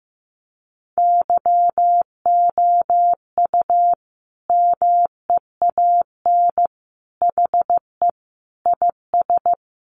Morse code, heard as a keyed tone, about 15 wpm, 700 Hz. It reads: YOU MEAN HE IS
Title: Sentences from Top 1200 Words 15wpm